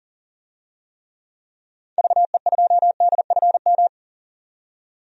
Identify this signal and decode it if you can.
Morse code, heard as a keyed tone, about 40 wpm, 700 Hz. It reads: VE2DFM
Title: DXCC-100 Call Signs - 6 Characters - 40wpm